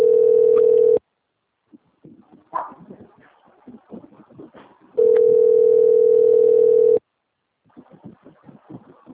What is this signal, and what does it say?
Ringback tone (North America), 440 + 480 Hz.
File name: out- - - - - .28479